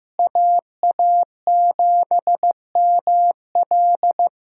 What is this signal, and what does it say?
Morse code, heard as a keyed tone, about 15 wpm, 700 Hz. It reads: AA7ML
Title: USA Call Signs - 5 Characters - Repeated 3x - 15wpm